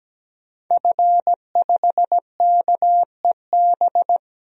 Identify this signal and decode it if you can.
Morse code, heard as a keyed tone, about 17 wpm, 700 Hz. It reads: F5KEB
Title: DXCC-100 Call Signs - 5 Characters - 17wpm